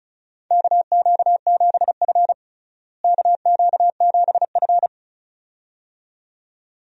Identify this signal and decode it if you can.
Morse code, heard as a keyed tone, about 36 wpm, 700 Hz. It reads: KQ7F KQ7F
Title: USA Call Signs - 4 Characters - Repeated 2x - 35wpm